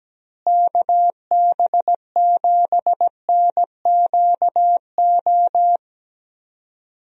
Morse code, heard as a keyed tone, about 17 wpm, 700 Hz. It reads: KB7NQO
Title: USA Call Signs - 6 Characters - Repeated 3x - 17wpm